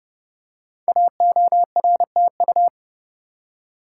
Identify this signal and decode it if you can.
Morse code, heard as a keyed tone, about 30 wpm, 700 Hz. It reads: AORTU
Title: Code Groups - 5 Letters - 30wpm